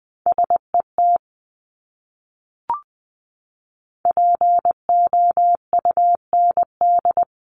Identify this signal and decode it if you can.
Morse code, heard as a keyed tone, about 20 wpm, 700 Hz. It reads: SET POUND
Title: Top 500 Words Repeated 20wpm